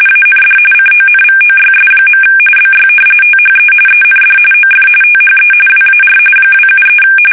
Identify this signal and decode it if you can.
RTTY, about 45.45 baud. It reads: AND CONVOYS. BY THE END OF THE WAR 273
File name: fsk850